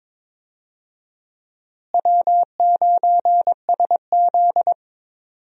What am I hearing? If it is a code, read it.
Morse code, heard as a keyed tone, about 22 wpm, 700 Hz. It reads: W9SZ
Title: USA Call Signs - 4 Characters - Repeated 3x - 22wpm